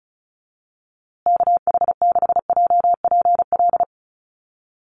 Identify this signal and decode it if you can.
Morse code, heard as a keyed tone, about 34 wpm, 700 Hz. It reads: KH6JPL
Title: USA Call Signs Repeated 3x 35wpm